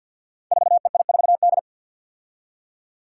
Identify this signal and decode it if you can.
Morse code, heard as a keyed tone, about 48 wpm, 700 Hz. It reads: 4EI4D
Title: Five Challenging Letter-Number Combinations 50wpm